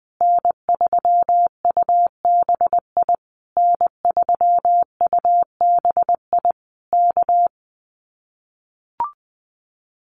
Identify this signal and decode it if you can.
Morse code, heard as a keyed tone, about 20 wpm, 700 Hz. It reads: N3UBI N3UBI K
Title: Common QSO Phrases Encore 20wpm